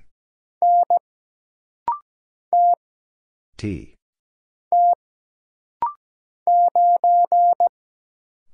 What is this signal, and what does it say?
Morse code, heard as a keyed tone, about 17 wpm, 700 Hz. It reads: N T T 9
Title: Single Letter-Number - Rapid-Fire 17wpm